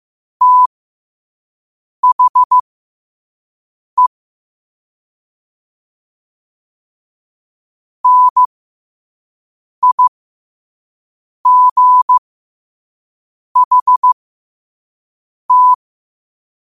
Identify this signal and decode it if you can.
Morse code, heard as a keyed tone, about 15 wpm, spaced as for 5 wpm, 1000 Hz. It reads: THE NIGHT
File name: qotdfiles / 5wpm